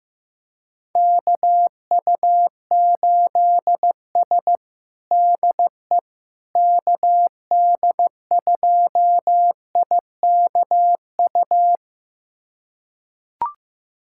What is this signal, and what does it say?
Morse code, heard as a keyed tone, about 15 wpm, 700 Hz. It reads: KU8S DE KD2IKU
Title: USA Callsign DE Callsign 15wpm